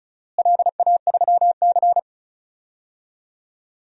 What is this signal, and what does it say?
Morse code, heard as a keyed tone, about 36 wpm, 700 Hz. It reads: LA3C
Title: DXCC-100 Call Signs - 4 Characters - Repeated 2x - 35wpm